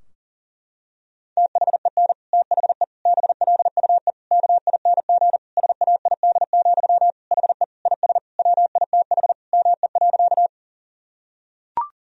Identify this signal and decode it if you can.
Morse code, heard as a keyed tone, about 40 wpm, 700 Hz. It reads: THEN THE BLUE KING SAID, HE IS WITH ME.
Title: Sentences from Top 1000 Words - Review Part 3 - 40wpm